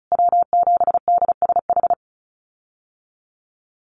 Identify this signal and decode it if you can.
Morse code, heard as a keyed tone, about 34 wpm, 700 Hz. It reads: W7DSH
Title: USA Call Signs Repeated 3x 35wpm